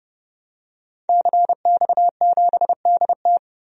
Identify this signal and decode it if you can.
Morse code, heard as a keyed tone, about 31 wpm, 700 Hz. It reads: CX7DT
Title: DXCC-100 Call Signs - 5 Characters - 30wpm